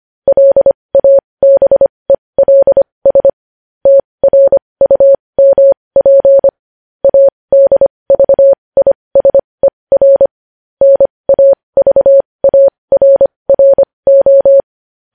Morse code, heard as a keyed tone, about 25 wpm, spaced as for 20 wpm, 550 Hz. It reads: LABELS TRUMP ADVISER NAVARRO
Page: news headlines in morse code at 20 wpm